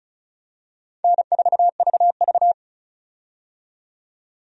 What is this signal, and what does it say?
Morse code, heard as a keyed tone, about 36 wpm, 700 Hz. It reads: N4VV
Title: USA Call Signs - 4 Characters - 35wpm